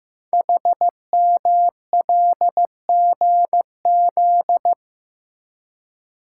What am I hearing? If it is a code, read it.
Morse code, heard as a keyed tone, about 15 wpm, 700 Hz. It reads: HMLGZ
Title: Code Groups - 5 Letters - 15wpm